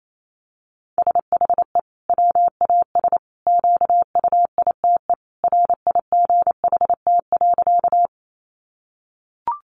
Morse code, heard as a keyed tone, about 28 wpm, 700 Hz. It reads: SHE WAS QUITE RIGHT.
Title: Sentences from Top 700 Words 28wpm